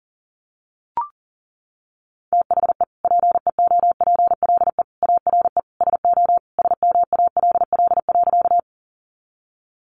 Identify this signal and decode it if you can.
Morse code, heard as a keyed tone, about 40 wpm, 700 Hz. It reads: THE PEOPLE ARE SO SMALL.
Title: Additional Easy Phrases 40wpm